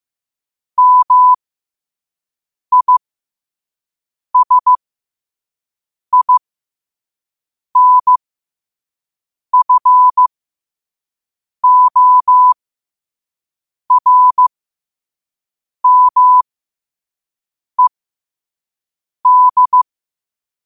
Morse code, heard as a keyed tone, about 15 wpm, spaced as for 5 wpm, 1000 Hz. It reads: MISINFORMED
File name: qotdfiles / 5wpm